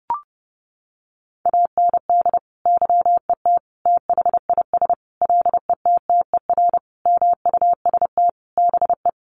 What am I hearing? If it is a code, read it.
Morse code, heard as a keyed tone, about 30 wpm, 700 Hz. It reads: AND YET THIS LETTER MUST BE
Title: Sentences from Top 1400 Words 30wpm